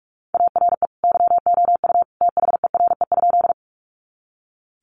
Morse code, heard as a keyed tone, about 44 wpm, 700 Hz. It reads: ARE YOU THERE?
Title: Sentences from Top 100 Words 45wpm